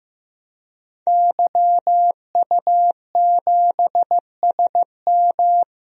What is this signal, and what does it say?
Morse code, heard as a keyed tone, about 15 wpm, 700 Hz. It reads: YU7SM
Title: DXCC-100 Call Signs - 5 Characters - Repeated 3x - 15wpm